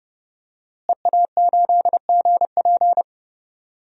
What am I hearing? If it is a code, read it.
Morse code, heard as a keyed tone, about 30 wpm, 700 Hz. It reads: EA8GP
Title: DXCC-100 Call Signs - 5 Characters - Repeated 2x - 30wpm